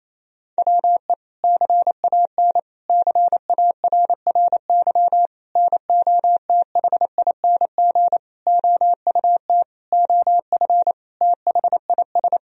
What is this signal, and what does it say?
Morse code, heard as a keyed tone, about 28 wpm, 700 Hz. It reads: WE CAN CARRY NOTHING OUT OF THIS
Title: Sentences from Top 1000 Words - Review Part 1 - 28wpm